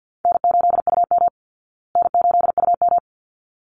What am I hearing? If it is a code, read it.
Morse code, heard as a keyed tone, about 51 wpm, 700 Hz. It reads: N8UM N8UM
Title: Easy Call Signs Repeated 2x - 50wpm